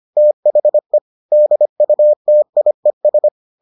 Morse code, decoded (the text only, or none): THE DUTIES